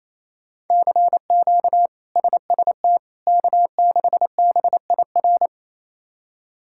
Morse code, heard as a keyed tone, about 28 wpm, 700 Hz. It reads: CQ SST K6BIR
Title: K1USN SST QSO Practice - 28wpm